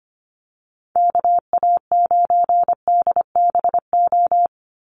Morse code, heard as a keyed tone, about 25 wpm, 700 Hz. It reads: KA9DBO
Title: USA Call Signs 25wpm